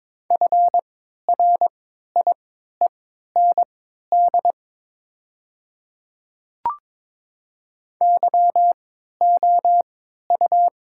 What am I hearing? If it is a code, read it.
Morse code, heard as a keyed tone, about 22 wpm, spaced as for 13 wpm, 700 Hz. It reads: FRIEND YOU
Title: 3x Character Spacing - Top 300 Words - 22wpm